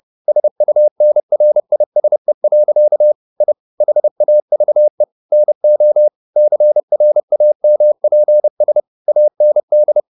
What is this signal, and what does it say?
Morse code, heard as a keyed tone, about 30 wpm, 600 Hz. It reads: SUNRISE. I HAVE NO CRAMPS AND
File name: the old man and the sea / the old man and the sea 1 30wpm